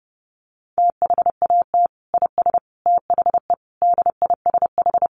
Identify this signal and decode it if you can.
Morse code, heard as a keyed tone, about 30 wpm, 700 Hz. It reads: THAT IS THE DISH
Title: Sentences from Top 1300 Words 30wpm